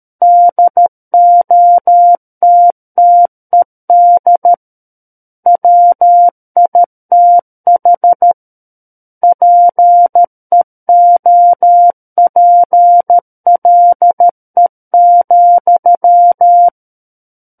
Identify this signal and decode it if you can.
Morse code, heard as a keyed tone, about 13 wpm, 700 Hz. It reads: DOTTED WITH PEOPLE,
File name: Book One / 04-Chapter 4 - 13 WPM